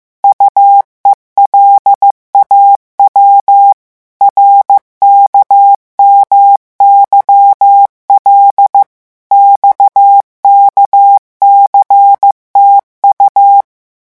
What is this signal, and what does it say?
Morse code, heard as a keyed tone, about 15 wpm, 800 Hz. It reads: UELAW RKMYL XKCTU
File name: morse mp3 / revision7-vitesse 14 mots